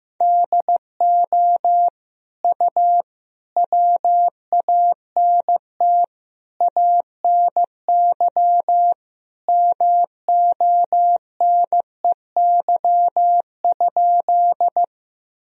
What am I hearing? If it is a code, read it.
Morse code, heard as a keyed tone, about 16 wpm, 700 Hz. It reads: DO U WANT ANY MONEY?
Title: Sentences from Top 400 Words and Abbreviations 15wpm